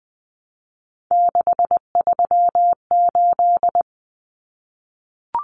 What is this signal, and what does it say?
Morse code, heard as a keyed tone, about 20 wpm, 700 Hz. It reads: 638
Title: Numbers - 3-Digit 20wpm